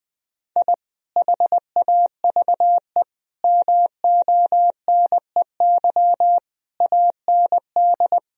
Morse code, heard as a keyed tone, about 21 wpm, 700 Hz. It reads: I HAVE MONEY AND
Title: Sentences from Top 1000 Words - Review Part 3 - 20wpm